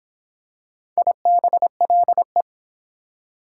Morse code, heard as a keyed tone, about 22 wpm, 700 Hz. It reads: IBLE